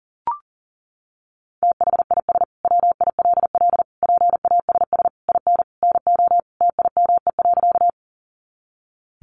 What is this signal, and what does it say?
Morse code, heard as a keyed tone, about 40 wpm, 700 Hz. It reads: THIS WILL PASS IN NO TIME.
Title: Sentences from Top 400 Words 40wpm